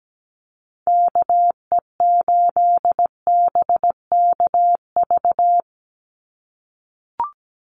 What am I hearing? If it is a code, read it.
Morse code, heard as a keyed tone, about 17 wpm, 700 Hz. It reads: KE8BKV
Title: USA Call Signs 17wpm